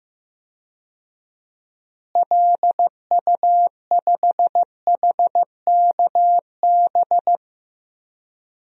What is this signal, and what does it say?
Morse code, heard as a keyed tone, about 15 wpm, 700 Hz. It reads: LU5HKB
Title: DXCC-100 Call Signs - 6 Characters - Repeated 3x - 15wpm